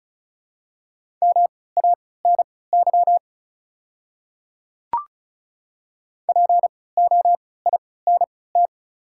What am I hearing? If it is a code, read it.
Morse code, heard as a keyed tone, about 34 wpm, spaced as for 20 wpm, 690 Hz. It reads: MANY POINT
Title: 3x Character Spacing - Top 300 Words - 35wpm